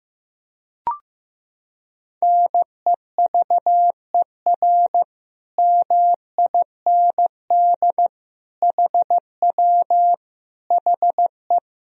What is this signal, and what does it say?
Morse code, heard as a keyed tone, about 15 wpm, 700 Hz. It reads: NEVER MIND HW HE
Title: Sentences from Top 500 Words and Abbreviations 15wpm